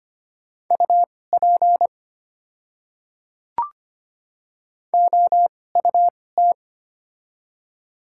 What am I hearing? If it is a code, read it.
Morse code, heard as a keyed tone, about 25 wpm, spaced as for 18 wpm, 700 Hz. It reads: UP OUT
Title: 2x Character Spacing - Top 300 Words - 25wpm